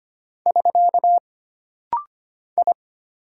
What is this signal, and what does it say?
Morse code, heard as a keyed tone, about 25 wpm, 700 Hz. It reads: <SK> I
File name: Single Character (Letter:Number:Punctuation:Prosign) - Rapid-Fire - 25wpm